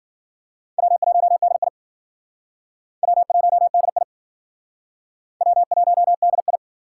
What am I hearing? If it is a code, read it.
Morse code, heard as a keyed tone, about 51 wpm, 700 Hz. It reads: W1DI W1DI W1DI